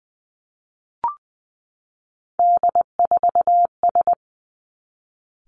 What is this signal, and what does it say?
Morse code, heard as a keyed tone, about 20 wpm, 700 Hz. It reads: D4S
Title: Three Challenging Letter-Number Combinations 20wpm